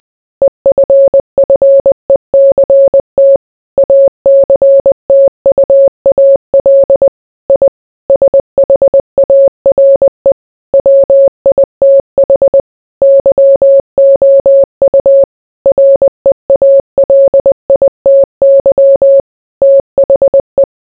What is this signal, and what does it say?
Morse code, heard as a keyed tone, about 20 wpm, 550 Hz. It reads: EFFECT ACTUAL I SHARE WITH YOU REALITY THE